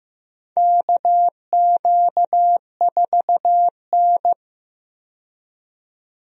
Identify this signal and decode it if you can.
Morse code, heard as a keyed tone, about 15 wpm, 700 Hz. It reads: KQ4N
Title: USA Call Signs - 4 Characters - 15wpm